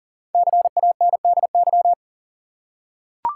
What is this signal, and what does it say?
Morse code, heard as a keyed tone, about 40 wpm, 700 Hz. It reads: CANDY
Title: Five Letter Words - 40wpm